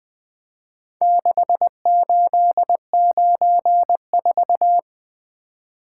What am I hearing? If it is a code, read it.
Morse code, heard as a keyed tone, about 21 wpm, 700 Hz. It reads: 6894
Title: Four Letter-Number Combinations - 20wpm